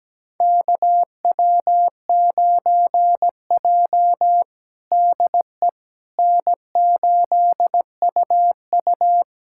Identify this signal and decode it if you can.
Morse code, heard as a keyed tone, about 17 wpm, 700 Hz. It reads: KW9J DE N8UU
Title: USA Callsign DE Callsign - 4 Characters - 17wpm